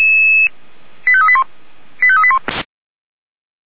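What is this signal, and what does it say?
Five-tone selective call (ZVEI), heard as quick runs of five tones, tones of about 70 ms. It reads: 85381 85381
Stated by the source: Received by SDR, ZVEI five-tone call 85371.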